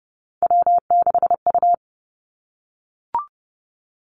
Morse code, heard as a keyed tone, about 30 wpm, 700 Hz. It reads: W6U
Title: Three Challenging Letter-Number Combinations 30wpm